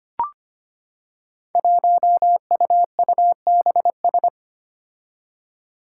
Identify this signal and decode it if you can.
Morse code, heard as a keyed tone, about 25 wpm, 700 Hz. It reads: 1UUBS
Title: Five Challenging Letter-Number Combinations 25wpm